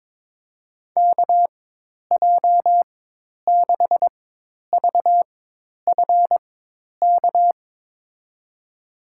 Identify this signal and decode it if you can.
Morse code, heard as a keyed tone, about 22 wpm, spaced as for 10 wpm, 700 Hz. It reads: KJ6VFK
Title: USA Call Signs - 6 Characters - 4x Character Spacing - 22wpm